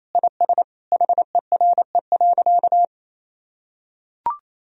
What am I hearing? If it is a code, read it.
Morse code, heard as a keyed tone, about 28 wpm, 700 Hz. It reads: IS HERE.
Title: Sentences from Top 1000 Words - Review Part 2 - 28wpm